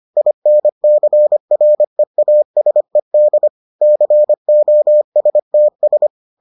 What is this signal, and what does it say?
Morse code, heard as a keyed tone, about 25 wpm, 600 Hz. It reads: INCREASED COSTS